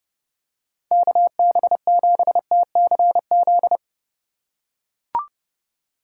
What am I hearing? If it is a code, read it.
Morse code, heard as a keyed tone, about 31 wpm, 700 Hz. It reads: KB7TCZ